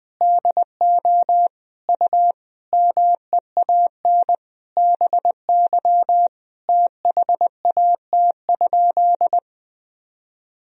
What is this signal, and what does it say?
Morse code, heard as a keyed tone, about 21 wpm, 700 Hz. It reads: DO U MEAN BY THAT?
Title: Sentences from Top 200 Words and Abbreviations 20wpm